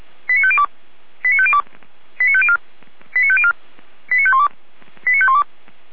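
Five-tone selective call (ZVEI), heard as quick runs of five tones, tones of about 70 ms. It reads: 89502 89502 89504 89504 89512 89512
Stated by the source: ZVEI five-tone selective call received by SDR, served by Feuerwehr Orlamünde.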